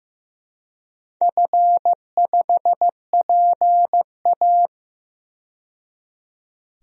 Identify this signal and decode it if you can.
Morse code, heard as a keyed tone, about 15 wpm, 700 Hz. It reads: F5PA
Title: DXCC-100 Call Signs - 4 Characters - 15wpm